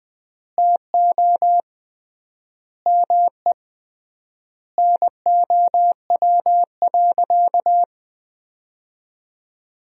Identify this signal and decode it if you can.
Morse code, heard as a keyed tone, about 20 wpm, 700 Hz. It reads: TO ME NOW.
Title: 3x Word-Spacing - Sentences from Top 500 Words - 20wpm